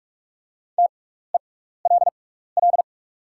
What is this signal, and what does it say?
Morse code, heard as a keyed tone, about 44 wpm, spaced as for 15 wpm, 690 Hz. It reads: TELL
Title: 6x Character Spacing - Top 200 Words - 45wpm